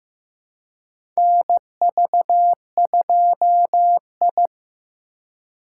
Morse code, heard as a keyed tone, about 16 wpm, 700 Hz. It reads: NV2I
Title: USA Call Signs - 4 Characters - 3x Character Spacing - 15wpm